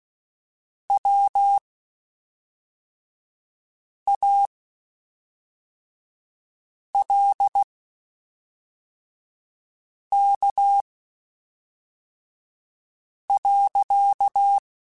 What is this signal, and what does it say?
Morse code, heard as a keyed tone, about 16 wpm, spaced as for 3 wpm, 800 Hz. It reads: WALK.
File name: Quotes Spaced / Quotes Spaced-mp3s / Temper sp @5